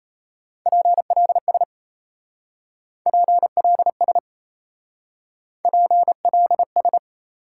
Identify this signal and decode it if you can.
Morse code, heard as a keyed tone, about 37 wpm, 700 Hz. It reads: PLS PLS PLS